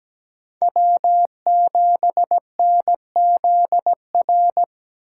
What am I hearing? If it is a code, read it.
Morse code, heard as a keyed tone, about 17 wpm, 700 Hz. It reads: W7NZR
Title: USA Call Signs - 5 Characters - 4x Character Spacing - 17wpm